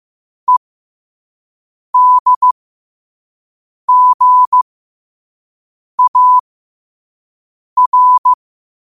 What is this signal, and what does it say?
Morse code, heard as a keyed tone, about 15 wpm, spaced as for 5 wpm, 1000 Hz. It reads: EDGAR
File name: qotdfiles / 5wpm